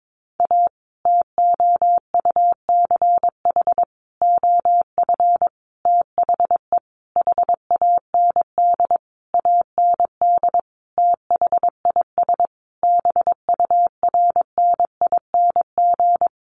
Morse code, heard as a keyed tone, about 22 wpm, 700 Hz. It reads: A TOUCH OF THE HAND AND THIS BURNING
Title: A Sound Of Thunder 22wpm